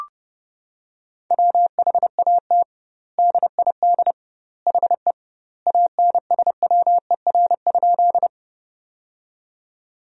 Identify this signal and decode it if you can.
Morse code, heard as a keyed tone, about 32 wpm, 700 Hz. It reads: WHAT DID HE ANSWER?
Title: 2x Word-Spacing - Sentences from Top 300 Words - 30wpm